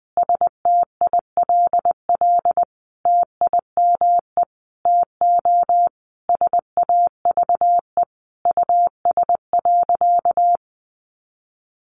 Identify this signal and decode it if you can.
Morse code, heard as a keyed tone, about 20 wpm, 700 Hz. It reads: STILL TIME TO SAVE US.
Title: Sentences from Top 1100 Words 20wpm